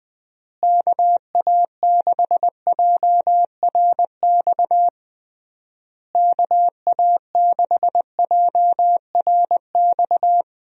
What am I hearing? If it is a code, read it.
Morse code, heard as a keyed tone, about 20 wpm, 700 Hz. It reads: KA6JRX KA6JRX
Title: USA Call Signs - 6 Characters - Repeated 3x - 20wpm